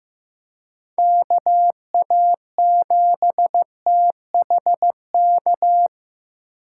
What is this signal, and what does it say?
Morse code, heard as a keyed tone, about 18 wpm, 700 Hz. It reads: KA7THK
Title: Speed Racing - USA 2x3 Call Signs 15wpm